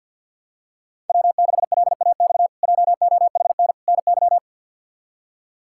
Morse code, heard as a keyed tone, about 51 wpm, 700 Hz. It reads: W6LAX JOHN NY
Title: K1USN SST QSO Practice - 50wpm